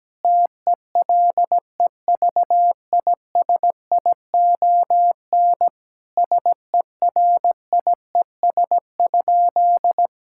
Morse code, heard as a keyed tone, about 17 wpm, 700 Hz. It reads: TELEVISION SERIES?
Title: Sentences from Top 1000 Words - Review Part 5 - 17wpm